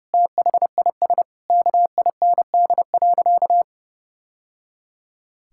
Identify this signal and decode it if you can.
Morse code, heard as a keyed tone, about 30 wpm, 700 Hz. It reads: THIS KIND.
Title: Sentences from Top 1000 Words - Review Part 5 - 30wpm